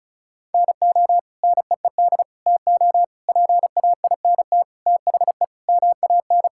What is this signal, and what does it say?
Morse code, heard as a keyed tone, about 34 wpm, 700 Hz. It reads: NO NEED TO PAINT THE MAN
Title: Sentences from Top 1000 Words - Review Part 6 - 35wpm